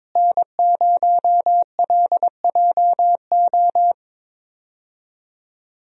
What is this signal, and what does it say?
Morse code, heard as a keyed tone, about 22 wpm, 700 Hz. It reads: N0LJO
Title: USA Call Signs - 5 Characters - Repeated 3x - 22wpm